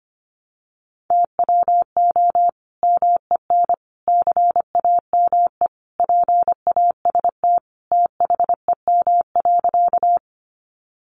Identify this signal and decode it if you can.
Morse code, heard as a keyed tone, about 25 wpm, 700 Hz. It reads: TWO MEN CAME PAST THEM.
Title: Sentences from Top 600 Words 25wpm